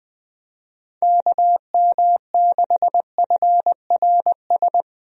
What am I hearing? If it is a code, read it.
Morse code, heard as a keyed tone, about 20 wpm, 700 Hz. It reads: KM6FRS
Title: USA Call Signs - 6 Characters - Repeated 3x - 20wpm